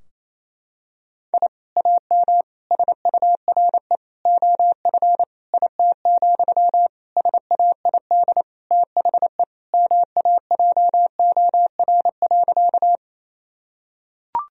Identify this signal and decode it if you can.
Morse code, heard as a keyed tone, about 28 wpm, 700 Hz. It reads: I AM SURE OF IT, SAID THE MAJOR.
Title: Sentences from Top 1000 Words - Review Part 4 - 28wpm